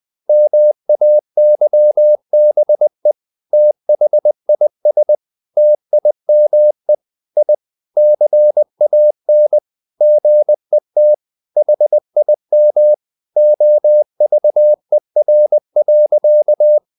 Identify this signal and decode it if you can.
Morse code, heard as a keyed tone, about 20 wpm, 600 Hz. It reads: MAYBE THIS TIME I CAN GET HIM OVER.
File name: the old man and the sea part 2 20wpm